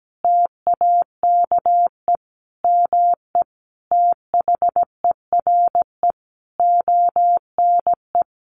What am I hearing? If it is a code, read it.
Morse code, heard as a keyed tone, about 17 wpm, 700 Hz. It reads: TAKE ME THERE ONE
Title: Sentences from Top 300 Words 17wpm